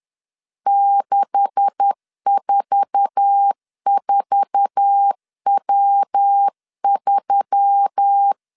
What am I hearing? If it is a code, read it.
Morse code, heard as a keyed tone, about 10 wpm, 800 Hz. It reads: 644W3
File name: CW / DET / 1 9 1500C S B 10wpm 4439